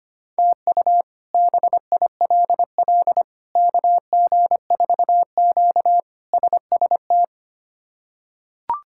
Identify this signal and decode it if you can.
Morse code, heard as a keyed tone, about 25 wpm, 700 Hz. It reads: TU BILL KG4Q SST
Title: K1USN SST QSO Practice - 25wpm